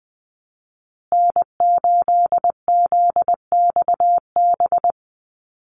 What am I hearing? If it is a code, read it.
Morse code, heard as a keyed tone, about 20 wpm, 700 Hz. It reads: N8ZXB
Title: USA Call Signs Repeated 3x 20wpm